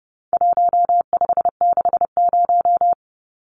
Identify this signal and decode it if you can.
Morse code, heard as a keyed tone, about 30 wpm, 700 Hz. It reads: 1560
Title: Numbers - Contest Style - 4-Digit 30wpm